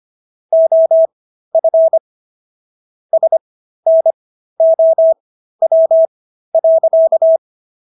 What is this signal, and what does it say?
Morse code, heard as a keyed tone, about 25 wpm, spaced as for 13 wpm, 650 Hz. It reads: OF SNOW.